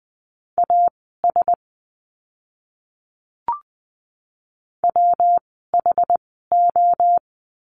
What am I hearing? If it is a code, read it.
Morse code, heard as a keyed tone, about 19 wpm, spaced as for 14 wpm, 700 Hz. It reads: AS WHO
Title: 160 Lesson 26f - Letter g - Words 20wpm